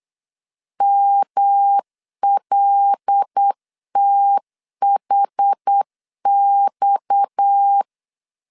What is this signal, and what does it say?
Morse code, heard as a keyed tone, about 8 wpm, 800 Hz. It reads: MLTHX
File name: CW / DET / 1 6 1500C S B 8wpm 5142